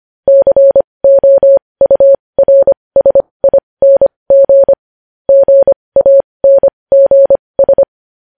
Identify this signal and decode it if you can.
Morse code, heard as a keyed tone, about 25 wpm, spaced as for 20 wpm, 550 Hz. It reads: COURSING GANGS
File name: podcast / News headlines-1x-25x20